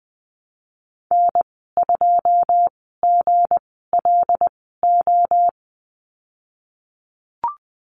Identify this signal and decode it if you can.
Morse code, heard as a keyed tone, about 20 wpm, spaced as for 15 wpm, 700 Hz. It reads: N2GLO
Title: 161 Lesson 26g - Letter g - Call Signs 20wpm